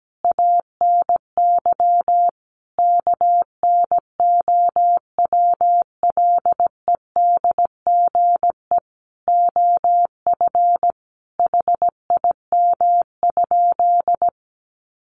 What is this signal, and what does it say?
Morse code, heard as a keyed tone, about 17 wpm, 700 Hz. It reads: ANY KNOWLEDGE OF HIM?
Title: Sentences from Top 600 Words 17wpm